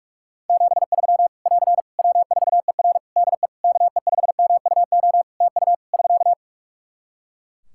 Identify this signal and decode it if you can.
Morse code, heard as a keyed tone, about 42 wpm, 700 Hz. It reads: 73 <AR> W4ER DE KE5MUO TU <SK>
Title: K1USN SST QSO Practice 45wpm